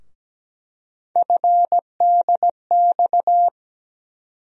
Morse code, heard as a keyed tone, about 17 wpm, 700 Hz. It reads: FDX
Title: USA Call Sign Postfixes - 3 Letters 17wpm